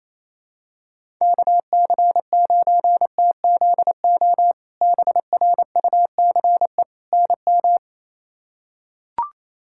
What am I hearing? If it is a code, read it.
Morse code, heard as a keyed tone, about 27 wpm, 700 Hz. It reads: KC9TZO BRUCE NM
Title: K1USN SST QSO Practice - 28wpm